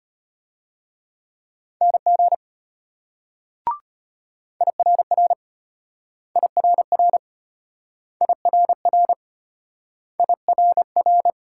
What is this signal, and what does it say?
Morse code, heard as a keyed tone, about 32 wpm, 700 Hz. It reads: NG IRR IRR IRR IRR